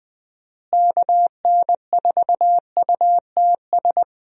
Morse code, heard as a keyed tone, about 20 wpm, 700 Hz. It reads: KN4UTS
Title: USA Call Signs - 6 Characters - Repeated 3x - 20wpm